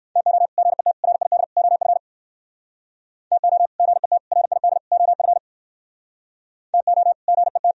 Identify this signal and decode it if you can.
Morse code, heard as a keyed tone, about 61 wpm, 700 Hz. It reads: TO GET RID OF TO GET RID OF TO GET